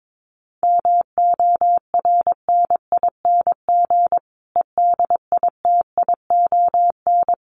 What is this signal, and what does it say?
Morse code, heard as a keyed tone, about 22 wpm, 700 Hz. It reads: MORNING EDITION
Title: Sets of 5 Words Encore 22wpm